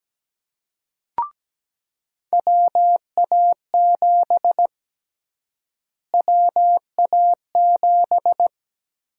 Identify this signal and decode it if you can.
Morse code, heard as a keyed tone, about 17 wpm, 700 Hz. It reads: WA7 WA7
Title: USA Call Sign Prefixes - Repeated 2x - 17wpm